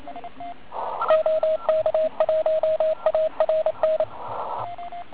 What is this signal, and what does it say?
Morse code, heard as a keyed tone, about 27 wpm, 630 Hz, with radio noise. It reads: OK1ARN
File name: Archiv / CUC138 / OK1IF CUC138